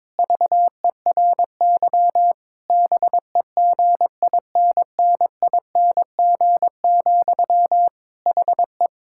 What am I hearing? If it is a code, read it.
Morse code, heard as a keyed tone, about 22 wpm, 700 Hz. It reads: VERY BEGINNING, HE
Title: Sentences from Top 1600 Words 22wpm